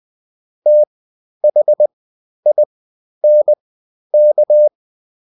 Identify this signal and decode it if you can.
Morse code, heard as a keyed tone, about 20 wpm, spaced as for 11 wpm, 600 Hz. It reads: THINK
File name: Top-100-Words-10